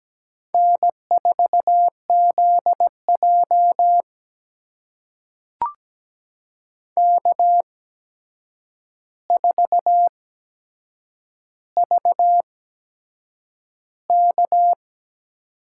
Morse code, heard as a keyed tone, about 17 wpm, spaced as for 7 wpm, 700 Hz. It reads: N4ZJ K4VK
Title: USA Call Signs - 4 Characters - 8x Character Spacing - 17wpm